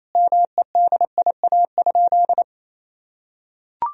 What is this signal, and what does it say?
Morse code, heard as a keyed tone, about 28 wpm, 700 Hz. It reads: MEDIA?
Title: Sentences from Top 1000 Words - Review Part 4 - 28wpm